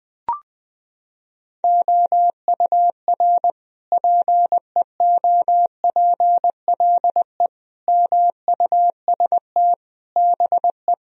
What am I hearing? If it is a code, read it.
Morse code, heard as a keyed tone, about 21 wpm, 700 Hz. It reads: OUR PEOPLE MUST BE
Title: Sentences from Top 1000 Words - Review Part 4 - 20wpm